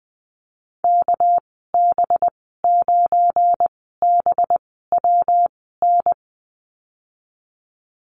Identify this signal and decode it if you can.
Morse code, heard as a keyed tone, about 20 wpm, spaced as for 15 wpm, 700 Hz. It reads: KB9BWN
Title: 207 Lesson 33g - Letter j - Call Signs 20wpm